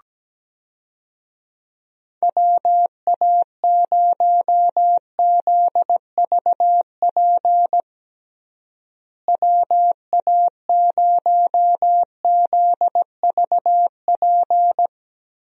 Morse code, heard as a keyed tone, about 17 wpm, 700 Hz. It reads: WA0ZVP WA0ZVP
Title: USA Call Signs - 6 Characters - Repeated 2x - 17wpm